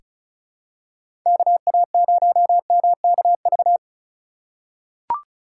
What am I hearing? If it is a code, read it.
Morse code, heard as a keyed tone, about 36 wpm, 700 Hz. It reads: KA0MKV
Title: USA Call Signs - 6 Characters - 2x Character Spacing - 35wpm